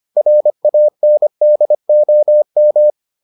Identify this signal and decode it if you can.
Morse code, heard as a keyed tone, about 25 wpm, 600 Hz. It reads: RANDOM